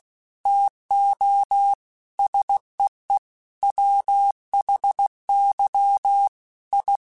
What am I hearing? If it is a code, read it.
Morse code, heard as a keyed tone, about 16 wpm, 800 Hz. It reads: TO SEE WHY I
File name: Technical Subjects / Technical Subjects-mp3s / OR Gate @15